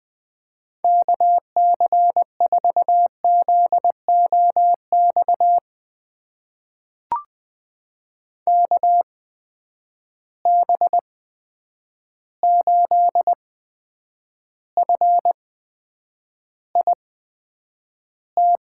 Morse code, heard as a keyed tone, about 20 wpm, spaced as for 9 wpm, 700 Hz. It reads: KC4ZOX KB8FIT
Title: USA Call Signs - 6 Characters - 8x Character Spacing - 20wpm